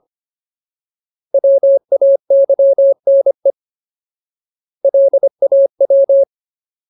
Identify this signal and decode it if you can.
Morse code, heard as a keyed tone, about 25 wpm, 550 Hz. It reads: WAYNE LAW
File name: Ultimate Word List / 25WPM Ultimate